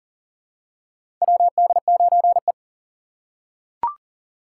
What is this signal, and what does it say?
Morse code, heard as a keyed tone, about 40 wpm, 700 Hz. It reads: WD9E
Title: USA Call Signs - 4 Characters - 4x Character Spacing - 40wpm